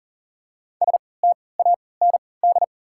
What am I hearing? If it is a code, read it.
Morse code, heard as a keyed tone, about 40 wpm, spaced as for 23 wpm, 700 Hz. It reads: STAND
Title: 3x Character Spacing - Top 300 Words - 40wpm